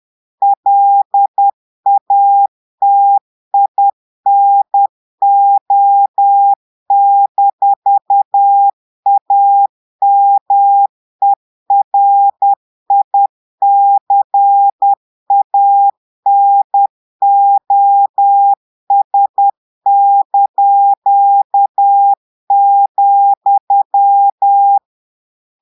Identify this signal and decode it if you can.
Morse code, heard as a keyed tone, about 10 wpm, 800 Hz. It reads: LATINO-AMERICANOS),
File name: Portugais - texte en clair 1 - vitesse - 10 mots minute